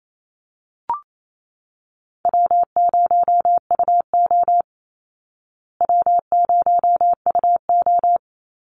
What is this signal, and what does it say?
Morse code, heard as a keyed tone, about 28 wpm, 700 Hz. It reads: W0UO W0UO
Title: Easy Call Signs Repeated 2x - 28wpm